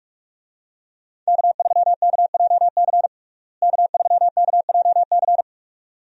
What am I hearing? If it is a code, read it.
Morse code, heard as a keyed tone, about 41 wpm, 700 Hz. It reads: K3KJC K3KJC
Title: USA Call Signs - 5 Characters - Repeated 2x - 45wpm